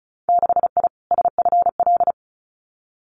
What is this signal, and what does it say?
Morse code, heard as a keyed tone, about 36 wpm, 700 Hz. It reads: 6I SFL
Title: Winter Field Day Exchanges Encore 35wpm